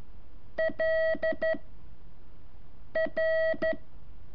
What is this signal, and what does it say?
Morse code, heard as a keyed tone, about 13 wpm, 640 Hz. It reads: L R
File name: curso / ejercicio 3